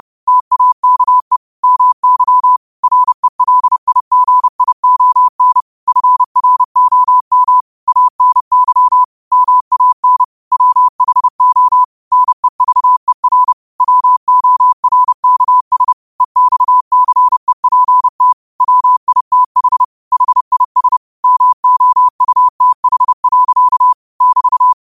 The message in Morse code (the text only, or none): TAKE MY RELIGION FROM ANY MAN WHO NEVER WORKS EXCEPT WITH HIS MOUTH. <BT>